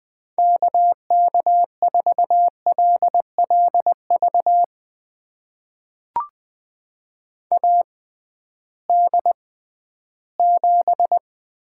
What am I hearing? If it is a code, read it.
Morse code, heard as a keyed tone, about 20 wpm, spaced as for 13 wpm, 700 Hz. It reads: KK4LLV AD7
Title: USA Call Signs - 6x Character Spacing - 20wpm